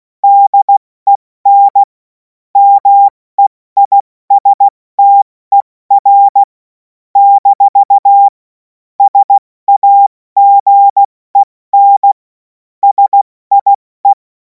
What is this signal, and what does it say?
Morse code, heard as a keyed tone, about 16 wpm, spaced as for 14 wpm, 800 Hz. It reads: DEN MEISTER - SAGEN SIE
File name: Morsekurs / Witze / Witz 21. April 2026 70 F80